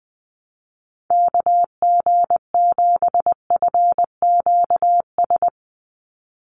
Morse code, heard as a keyed tone, about 20 wpm, 700 Hz. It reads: KG7FQS
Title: USA Call Signs 20wpm